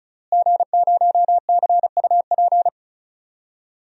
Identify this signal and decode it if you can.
Morse code, heard as a keyed tone, about 36 wpm, 700 Hz. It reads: G0CUP
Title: DXCC-100 Call Signs - 5 Characters - Repeated 3x - 35wpm